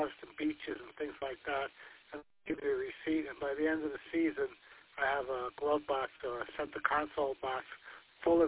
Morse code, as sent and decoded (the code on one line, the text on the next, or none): none